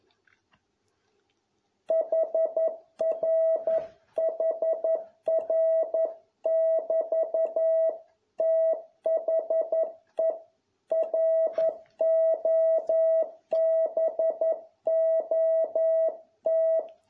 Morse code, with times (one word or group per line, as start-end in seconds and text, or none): 1.9-7.9 HRHR<BT>
8.4-10.3 THE
10.9-16.8 ROBOT